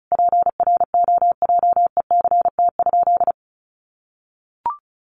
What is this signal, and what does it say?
Morse code, heard as a keyed tone, about 34 wpm, 700 Hz. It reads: PROJECT?
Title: Sentences from Top 1400 Words 35wpm